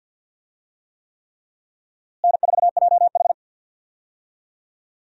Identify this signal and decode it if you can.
Morse code, heard as a keyed tone, about 51 wpm, 700 Hz. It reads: N4JH
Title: USA Call Signs - 4 Characters - Repeated 3x - 50wpm